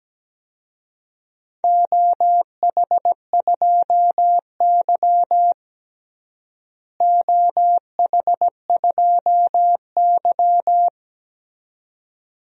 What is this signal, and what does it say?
Morse code, heard as a keyed tone, about 17 wpm, 700 Hz. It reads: OH2Y OH2Y
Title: DXCC-100 Call Signs - 4 Characters - Repeated 2x - 17wpm